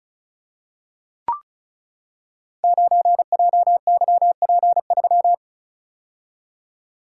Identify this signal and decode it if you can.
Morse code, heard as a keyed tone, about 36 wpm, 700 Hz. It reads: 9JYP3
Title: Code Groups - 5 Letter:Numbers - 35wpm